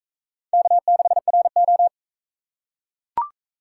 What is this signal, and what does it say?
Morse code, heard as a keyed tone, about 39 wpm, 700 Hz. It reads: K6RO